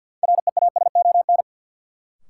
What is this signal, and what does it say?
Morse code, heard as a keyed tone, about 53 wpm, 700 Hz. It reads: REASON
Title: Sets of 5 Words Encore - New Word Drill - 50wpm